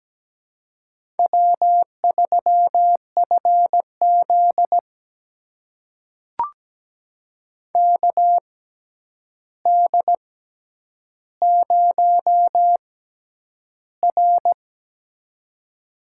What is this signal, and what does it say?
Morse code, heard as a keyed tone, about 17 wpm, spaced as for 9 wpm, 700 Hz. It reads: W3FZ KD0R
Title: USA Call Signs - 4 Characters - 6x Character Spacing - 17wpm